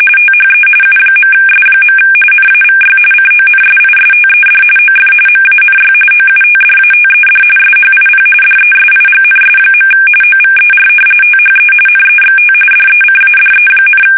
RTTY, about 45.45 baud. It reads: THE U.S. NAVY CONSISTED OF MANY OUTDATED SHIP DESIGNS. OVER THE NEXT DECADE,
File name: fsk850 / usnavy1